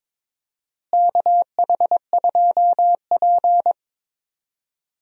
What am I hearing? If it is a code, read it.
Morse code, heard as a keyed tone, about 21 wpm, 700 Hz. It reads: KH2P
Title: USA Call Signs - 4 Characters - Repeated 2x - 22wpm